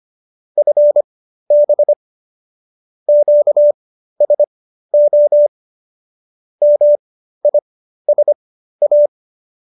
Morse code, heard as a keyed tone, about 25 wpm, spaced as for 13 wpm, 590 Hz. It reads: FB QSO MISA